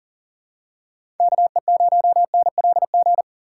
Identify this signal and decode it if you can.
Morse code, heard as a keyed tone, about 40 wpm, 700 Hz. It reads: KE0NLG
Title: USA Call Signs - 6 Characters - 6x Character Spacing - 40wpm